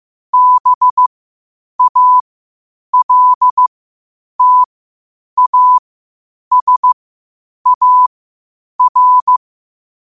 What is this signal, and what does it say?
Morse code, heard as a keyed tone, about 15 wpm, spaced as for 8 wpm, 1000 Hz. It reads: BALTASAR
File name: qotdfiles / 8wpm